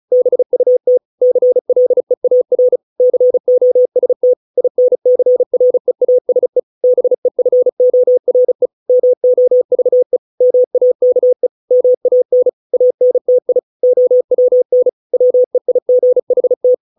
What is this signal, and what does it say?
Morse code, heard as a keyed tone, about 34 wpm, 500 Hz. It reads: BUT CLEAR COST INCREASE BEFORE MOVE MAKE MAN ANTI OWN WEIGHT